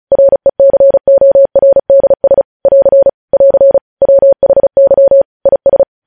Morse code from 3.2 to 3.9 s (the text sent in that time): <AR>